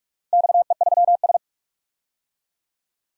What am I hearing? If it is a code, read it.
Morse code, heard as a keyed tone, about 42 wpm, 700 Hz. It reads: XE3S
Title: DXCC-100 Call Signs - 4 Characters - 45wpm